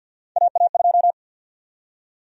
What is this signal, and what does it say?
Morse code, heard as a keyed tone, about 52 wpm, 700 Hz. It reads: AA2
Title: USA Call Sign Prefixes - 50wpm